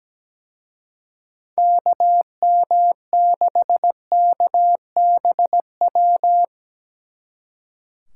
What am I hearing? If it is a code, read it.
Morse code, heard as a keyed tone, about 19 wpm, 700 Hz. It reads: KM6KBW
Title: Speed Racing - USA Call Signs 17wpm